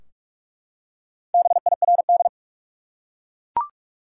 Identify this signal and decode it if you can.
Morse code, heard as a keyed tone, about 42 wpm, 700 Hz. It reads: BIRD
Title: Four Letter Words - 45wpm